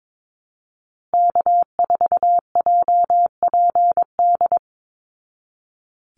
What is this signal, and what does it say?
Morse code, heard as a keyed tone, about 22 wpm, 700 Hz. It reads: K4JPD
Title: Call Signs Repeated 2x - 22wpm